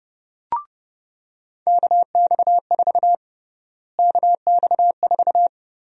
Morse code, heard as a keyed tone, about 30 wpm, 700 Hz. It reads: KX4 KX4
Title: USA Call Sign Prefixes - Repeated 2x - 30wpm